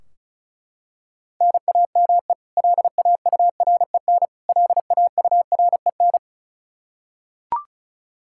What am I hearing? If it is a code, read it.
Morse code, heard as a keyed tone, about 34 wpm, 700 Hz. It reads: NAME LAUREN LAUREN
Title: Practice QSO - Exchange 1 - Beginning - 35wpm